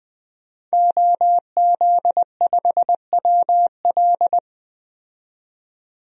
Morse code, heard as a keyed tone, about 20 wpm, 700 Hz. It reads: OZ5WL
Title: DXCC-100 Call Signs - 5 Characters - 20wpm